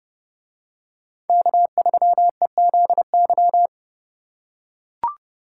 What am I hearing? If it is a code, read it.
Morse code, heard as a keyed tone, about 29 wpm, 700 Hz. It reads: K3EZY